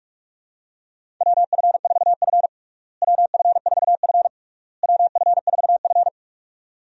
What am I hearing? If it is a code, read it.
Morse code, heard as a keyed tone, about 46 wpm, 700 Hz. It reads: WF4F WF4F WF4F
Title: USA Call Signs - 4 Characters - Repeated 3x - 45wpm